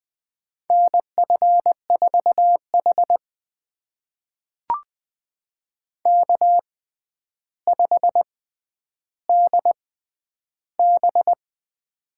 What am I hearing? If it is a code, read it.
Morse code, heard as a keyed tone, about 20 wpm, spaced as for 10 wpm, 700 Hz. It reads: NF4H K5DB
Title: USA Call Signs - 4 Characters - 6x Character Spacing - 20wpm